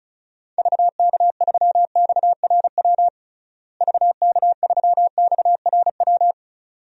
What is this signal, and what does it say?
Morse code, heard as a keyed tone, about 34 wpm, 700 Hz. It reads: VK3XRW VK3XRW
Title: DXCC-100 Call Signs - 6 Characters - Repeated 2x - 35wpm